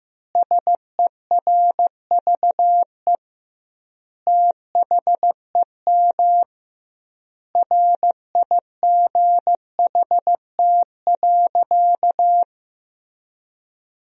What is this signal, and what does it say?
Morse code, heard as a keyed tone, about 16 wpm, 700 Hz. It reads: SERVE THEM RIGHT.
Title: 2x Word-Spacing - Sentences from Top 400 Words - 15wpm